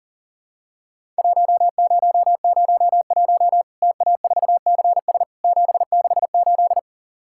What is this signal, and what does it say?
Morse code, heard as a keyed tone, about 40 wpm, 700 Hz. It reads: 1001 TA4CS 768
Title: Worked All Europe DX Contest - QTC Exchange - 40wpm